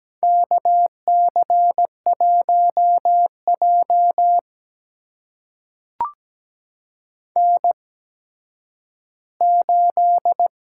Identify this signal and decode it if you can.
Morse code, heard as a keyed tone, about 17 wpm, spaced as for 10 wpm, 700 Hz. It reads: KC1J N8
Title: USA Call Signs - 8x Character Spacing - 17wpm